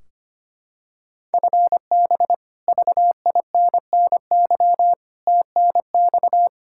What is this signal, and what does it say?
Morse code, heard as a keyed tone, about 25 wpm, 700 Hz. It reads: FB VINNY TNX
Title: Practice QSO - Exchange 3 - More Info - 25wpm